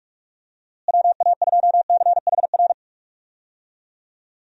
Morse code, heard as a keyed tone, about 46 wpm, 700 Hz. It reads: WA2CHR
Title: USA Call Signs - 6 Characters - Repeated 2x - 45wpm